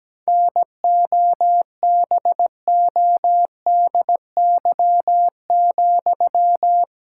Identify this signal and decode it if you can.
Morse code, heard as a keyed tone, about 17 wpm, 700 Hz. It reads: NOBODY,
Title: Sentences from Top 1600 Words 17wpm